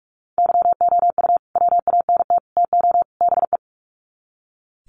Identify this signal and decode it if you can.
Morse code, heard as a keyed tone, about 44 wpm, 700 Hz. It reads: YOU WANT TO BE
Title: Sets of 4 Words 45wpm